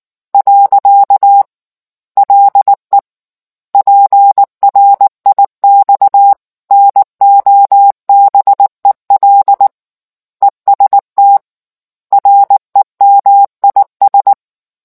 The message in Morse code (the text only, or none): . LE PRIX NOBEL EST REMIS